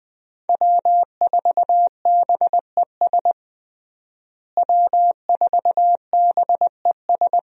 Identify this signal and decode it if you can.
Morse code, heard as a keyed tone, about 20 wpm, 700 Hz. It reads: W4BES W4BES
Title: USA Call Signs - 5 Characters - Repeated 2x - 20wpm